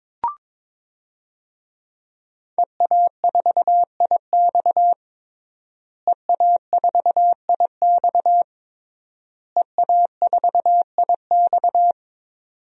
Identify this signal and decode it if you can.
Morse code, heard as a keyed tone, about 21 wpm, 700 Hz. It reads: EA4IX EA4IX EA4IX
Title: DXCC-100 Call Signs - 5 Characters - Repeated 3x - 22wpm